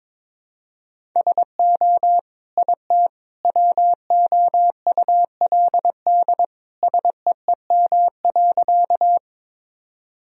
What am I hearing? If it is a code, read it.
Morse code, heard as a keyed tone, about 22 wpm, 700 Hz. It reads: SO IT WOULD SEEM.
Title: Sentences from Top 1000 Words - Review Part 1 - 22wpm